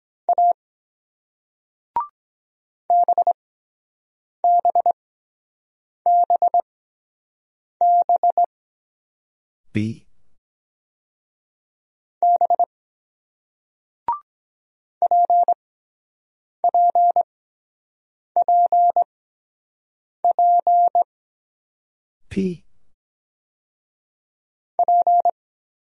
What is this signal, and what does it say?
Morse code, heard as a keyed tone, about 23 wpm, spaced as for 7 wpm, 700 Hz. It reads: A BBBB B PPPP P